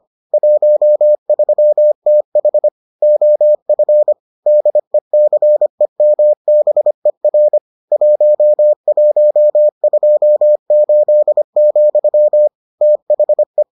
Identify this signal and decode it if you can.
Morse code, heard as a keyed tone, about 25 wpm, 600 Hz. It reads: 13TH OF DECEMBER 1128, THE